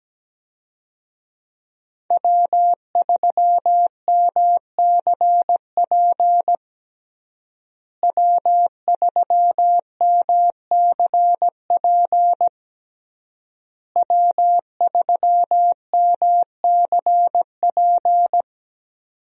Morse code, heard as a keyed tone, about 17 wpm, 700 Hz. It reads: W3MCP W3MCP W3MCP
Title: USA Call Signs - 5 Characters - Repeated 3x - 17wpm